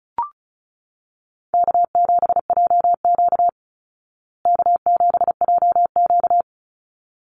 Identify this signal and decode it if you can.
Morse code, heard as a keyed tone, about 34 wpm, 700 Hz. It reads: K7JQ K7JQ
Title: Easy Call Signs Repeated 2x - 35wpm